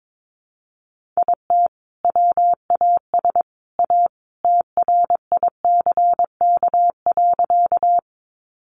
Morse code, heard as a keyed tone, about 22 wpm, 700 Hz. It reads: IT WAS A TRICK.
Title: Sentences from Top 1500 Words 22wpm